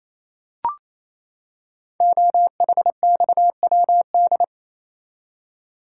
Morse code, heard as a keyed tone, about 27 wpm, 700 Hz. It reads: OHXWD
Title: Code Groups - 5 Letters - 28wpm